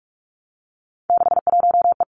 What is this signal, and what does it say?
Morse code, heard as a keyed tone, about 46 wpm, 700 Hz. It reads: 61E
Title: Three Challenging Letter-Number Combinations 45wpm